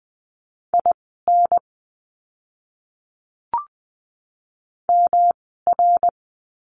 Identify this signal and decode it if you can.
Morse code, heard as a keyed tone, about 20 wpm, spaced as for 15 wpm, 700 Hz. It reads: IN MR
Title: 244 Lesson 39f - Letter z - Words 20wpm